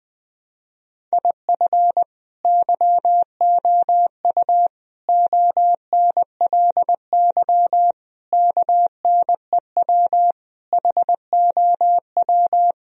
Morse code, heard as a keyed tone, about 20 wpm, 700 Hz. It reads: IF YOU ONLY KNEW HOW
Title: Sentences from Top 1000 Words - Review Part 4 - 20wpm